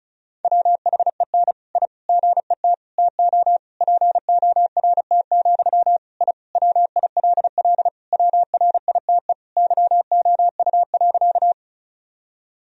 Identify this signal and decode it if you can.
Morse code, heard as a keyed tone, about 36 wpm, 700 Hz. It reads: WHEN I GET TO PORT, I WILL WRITE YOU.
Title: Sentences from Top 1000 Words - Review Part 1 - 35wpm